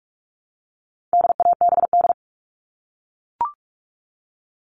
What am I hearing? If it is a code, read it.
Morse code, heard as a keyed tone, about 44 wpm, 700 Hz. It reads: DABD
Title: All Cut Numbers - 4-Digit 45wpm